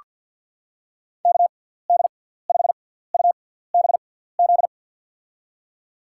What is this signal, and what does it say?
Morse code, heard as a keyed tone, about 48 wpm, spaced as for 17 wpm, 690 Hz. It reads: KD5UBZ
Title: USA Call Signs - 6x Character Spacing - 50wpm